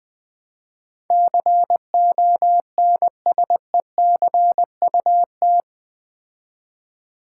Morse code, heard as a keyed tone, about 20 wpm, 700 Hz. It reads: CONSECUT
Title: Build A Word - 4 Letters At A Time - 11-Letter Words - 20wpm